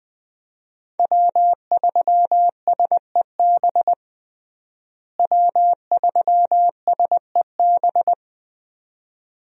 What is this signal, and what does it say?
Morse code, heard as a keyed tone, about 20 wpm, 700 Hz. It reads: W3SEB W3SEB
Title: USA Call Signs - 5 Characters - Repeated 2x - 20wpm